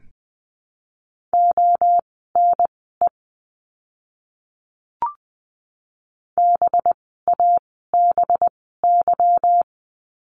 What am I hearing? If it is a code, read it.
Morse code, heard as a keyed tone, about 20 wpm, spaced as for 14 wpm, 700 Hz. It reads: ONE BABY
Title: 185 Lesson 30f - Letter b - Words 20wpm